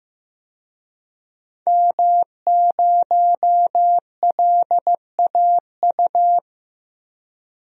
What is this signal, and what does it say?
Morse code, heard as a keyed tone, about 16 wpm, 700 Hz. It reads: M0LAU
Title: DXCC-100 Call Signs - 5 Characters - 15wpm